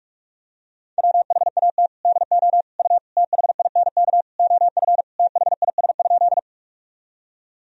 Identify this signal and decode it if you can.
Morse code, heard as a keyed tone, about 41 wpm, 700 Hz. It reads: WHAT DO U THINK OF THIS?
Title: Sentences from Top 200 Words and Abbreviations 45wpm